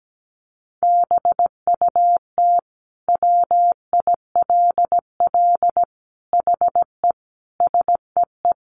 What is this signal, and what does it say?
Morse code, heard as a keyed tone, about 17 wpm, 700 Hz. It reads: BUT WILL HE SEE
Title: Sentences from Top 100 Words 17wpm